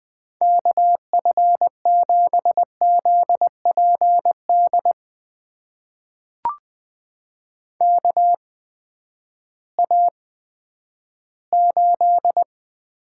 Morse code, heard as a keyed tone, about 20 wpm, spaced as for 11 wpm, 700 Hz. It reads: KF7ZPD KA8
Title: USA Call Signs - 6 Characters - 8x Character Spacing - 20wpm